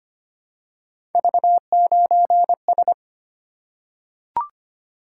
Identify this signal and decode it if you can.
Morse code, heard as a keyed tone, about 25 wpm, 700 Hz. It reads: V9S
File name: Three Letter-Number Combinations - 25wpm